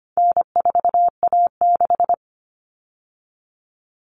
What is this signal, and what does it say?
Morse code, heard as a keyed tone, about 25 wpm, 700 Hz. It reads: N4A6
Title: All Cut Numbers - 4-Digit 25wpm